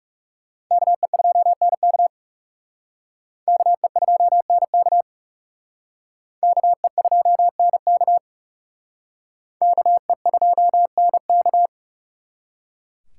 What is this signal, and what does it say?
Morse code, heard as a keyed tone, about 37 wpm, 700 Hz. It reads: KE2NK KE2NK KE2NK KE2NK
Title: Speed Racing - USA 2x2 Call Signs 30wpm